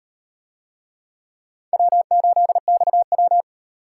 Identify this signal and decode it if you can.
Morse code, heard as a keyed tone, about 34 wpm, 700 Hz. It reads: W8XW